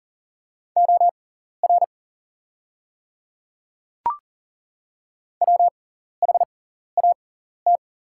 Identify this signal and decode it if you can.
Morse code, heard as a keyed tone, about 41 wpm, spaced as for 14 wpm, 690 Hz. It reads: OR WHAT
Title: 6x Character Spacing - Top 300 Words - 40wpm